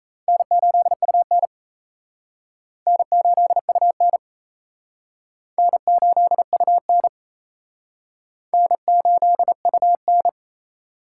Morse code, heard as a keyed tone, about 35 wpm, 700 Hz. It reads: N8UN N8UN N8UN N8UN